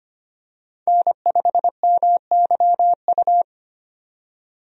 Morse code, heard as a keyed tone, about 25 wpm, 700 Hz. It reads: N5MYU
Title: USA Call Signs - 5 Characters - 25wpm